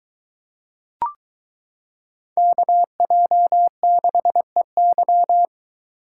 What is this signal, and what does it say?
Morse code, heard as a keyed tone, about 22 wpm, 700 Hz. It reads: KJ6EY